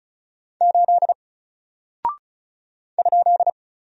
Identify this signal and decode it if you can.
Morse code, heard as a keyed tone, about 36 wpm, 700 Hz. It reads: 8 ?
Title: Single Character (Letter:Number:Punctuation:Prosign) 35wpm